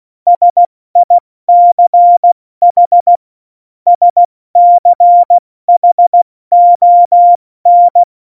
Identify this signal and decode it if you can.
Morse code, heard as a keyed tone, about 16 wpm, 700 Hz. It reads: SICH SCHON
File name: Morsekurs / Witze / Witz 21. Januar 2020, 70 F80